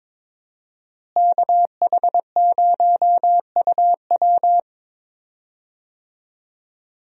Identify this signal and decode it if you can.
Morse code, heard as a keyed tone, about 22 wpm, 700 Hz. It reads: KH0UW
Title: USA Call Signs - 5 Characters - Repeated 3x - 22wpm